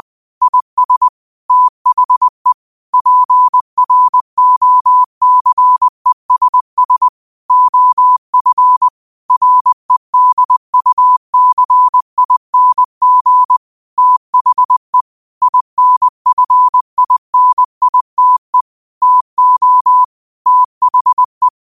Morse code, heard as a keyed tone, about 20 wpm, 1000 Hz. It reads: IS THE PROCESS OF REDUCING THE INFINITE TO THE
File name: qotdfiles / 20wpm